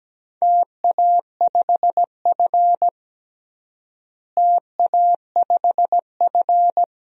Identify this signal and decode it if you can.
Morse code, heard as a keyed tone, about 17 wpm, 700 Hz. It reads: TA5F TA5F
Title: DXCC-100 Call Signs - 4 Characters - Repeated 2x - 17wpm